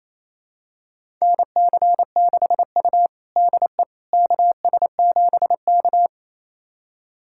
Morse code, heard as a keyed tone, about 28 wpm, 700 Hz. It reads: NC6U DE KS7K
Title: USA Callsign DE Callsign - 4 Characters - 28wpm